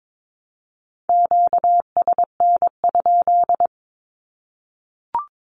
Morse code, heard as a keyed tone, about 22 wpm, 700 Hz. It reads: QSN?
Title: Q-Codes Repeated 3x 22wpm